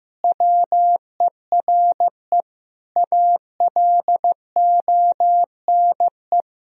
Morse code, heard as a keyed tone, about 16 wpm, 700 Hz. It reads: WERE ALONE
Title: Sentences from Top 1000 Words - Review Part 3 - 15wpm